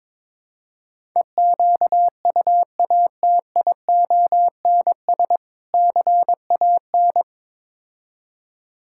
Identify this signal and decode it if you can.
Morse code, heard as a keyed tone, about 23 wpm, 700 Hz. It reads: EQUATIONS CAN
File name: practice / 22wpm / Build A Phrase - 5-Word Phrases - 22wpm